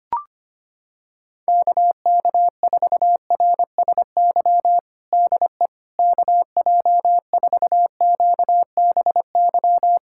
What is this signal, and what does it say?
Morse code, heard as a keyed tone, about 25 wpm, 700 Hz. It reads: KK4RSY DE KJ4QBY
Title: USA Callsign DE Callsign - 6 Characters - 25wpm